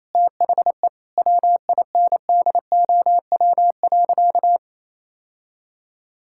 Morse code, heard as a keyed tone, about 28 wpm, 700 Hz. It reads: THE WINDOW.
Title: Sentences from Top 1600 Words 28wpm